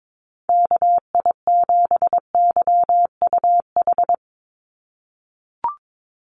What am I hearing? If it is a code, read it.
Morse code, heard as a keyed tone, about 22 wpm, 700 Hz. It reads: KI7YUH
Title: USA Call Signs 22wpm